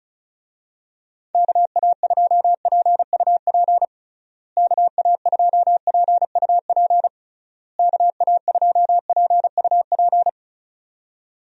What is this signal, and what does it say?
Morse code, heard as a keyed tone, about 36 wpm, 700 Hz. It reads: KA2PUP KA2PUP KA2PUP
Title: USA Call Signs - 6 Characters - Repeated 3x - 35wpm